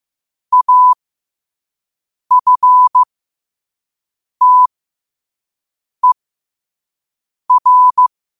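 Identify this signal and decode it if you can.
Morse code, heard as a keyed tone, about 15 wpm, spaced as for 5 wpm, 1000 Hz. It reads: AFTER
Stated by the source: Morse code sent at 5 words per minute.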